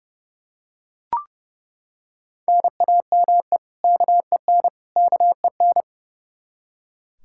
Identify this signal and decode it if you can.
Morse code, heard as a keyed tone, about 30 wpm, 700 Hz. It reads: NAME KEN KEN
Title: Practice QSO - Exchange 1 - Beginning - 30wpm